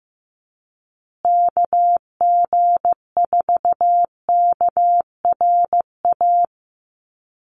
Morse code, heard as a keyed tone, about 15 wpm, 700 Hz. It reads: KG4KRA
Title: USA Call Signs Repeated 2x 15wpm